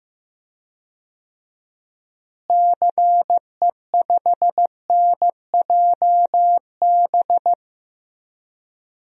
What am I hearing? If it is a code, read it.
Morse code, heard as a keyed tone, about 15 wpm, 700 Hz. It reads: CE5NJB
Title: DXCC-100 Call Signs - 6 Characters - Repeated 3x - 15wpm